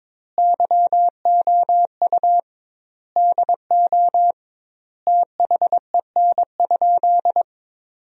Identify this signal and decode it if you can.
Morse code, heard as a keyed tone, about 22 wpm, 700 Hz. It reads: YOU DO THEN?
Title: 2x Word-Spacing - Sentences from Top 100 Words - 22wpm